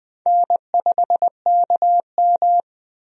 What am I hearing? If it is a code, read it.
Morse code, heard as a keyed tone, about 20 wpm, 700 Hz. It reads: N5KM
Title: USA Call Signs - 4 Characters - Repeated 3x - 20wpm